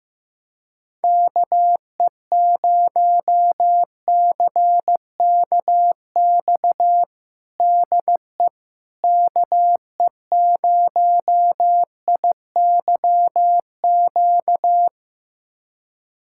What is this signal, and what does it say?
Morse code, heard as a keyed tone, about 15 wpm, 700 Hz. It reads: KE0CKX DE KE0IYQ
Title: USA Callsign DE Callsign - 6 Characters - 15wpm